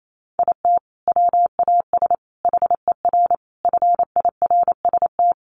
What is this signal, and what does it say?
Morse code, heard as a keyed tone, about 28 wpm, 700 Hz. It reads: IT WAS HER FIRST
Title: Sentences from Top 1100 Words 28wpm